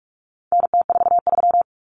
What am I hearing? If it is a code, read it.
Morse code, heard as a keyed tone, about 44 wpm, 700 Hz. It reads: NT43
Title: Common Cut Numbers - 4-Digit 45wpm